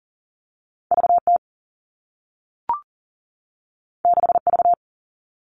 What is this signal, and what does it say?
Morse code, heard as a keyed tone, about 41 wpm, 700 Hz. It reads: VT 6V
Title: All Cut Numbers - 2-Digit 40wpm